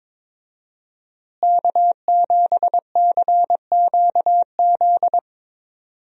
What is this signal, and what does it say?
Morse code, heard as a keyed tone, about 22 wpm, 700 Hz. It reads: K7CQZ
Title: USA Call Signs - 5 Characters - Repeated 3x - 22wpm